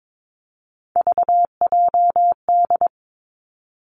Morse code, heard as a keyed tone, about 22 wpm, 700 Hz. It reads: VJD
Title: Three Challenging Letter-Number Combinations 22wpm